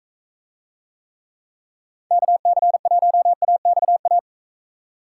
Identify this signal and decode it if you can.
Morse code, heard as a keyed tone, about 42 wpm, 700 Hz. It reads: KC1AXA